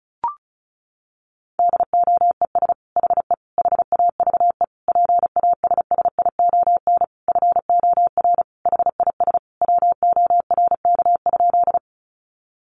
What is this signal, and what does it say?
Morse code, heard as a keyed tone, about 34 wpm, 700 Hz. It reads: DOES HE HAVE PASSION FOR HIS WORK?
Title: Sentences from Top 1300 Words 35wpm